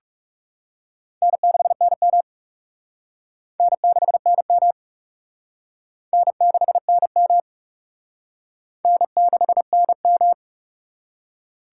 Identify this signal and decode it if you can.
Morse code, heard as a keyed tone, about 36 wpm, 700 Hz. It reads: N6NM N6NM N6NM N6NM
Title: Speed Racing - USA 1x2 Call Signs 30wpm